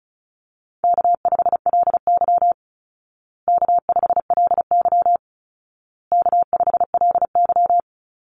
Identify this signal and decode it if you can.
Morse code, heard as a keyed tone, about 34 wpm, 700 Hz. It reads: K5LY K5LY K5LY
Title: Easy Call Signs Repeated 3x - 35wpm